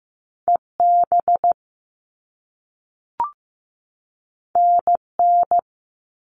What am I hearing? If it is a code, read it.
Morse code, heard as a keyed tone, about 15 wpm, 700 Hz. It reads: EB NN
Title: All Cut Numbers - 2-Digit 15wpm